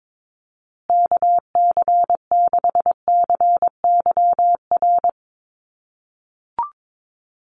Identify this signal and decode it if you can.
Morse code, heard as a keyed tone, about 22 wpm, 700 Hz. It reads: KC6CYR
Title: USA Call Signs Repeated 2x 22wpm